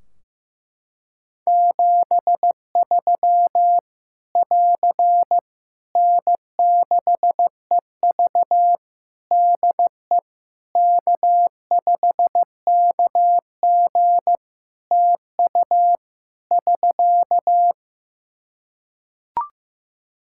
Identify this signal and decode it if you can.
Morse code, heard as a keyed tone, about 15 wpm, 700 Hz. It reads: 73 <AR> N6EV DE K5KG TU <SK>
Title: K1USN SST QSO Practice 15wpm